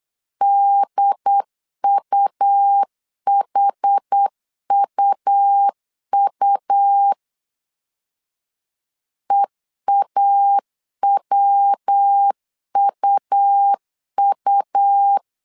Morse code, heard as a keyed tone, about 8 wpm, 800 Hz. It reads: DUHUU EAWUU
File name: CW / DET / 1 3 1500C S A 8wpm 4229